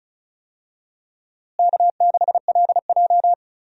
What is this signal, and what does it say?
Morse code, heard as a keyed tone, about 36 wpm, 700 Hz. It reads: K6LJ
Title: USA Call Signs - 4 Characters - Repeated 3x - 35wpm